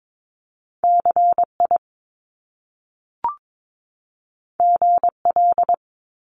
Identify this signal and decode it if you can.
Morse code, heard as a keyed tone, about 22 wpm, 700 Hz. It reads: CI GL
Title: Top Two Letter Combinations in Words 22wpm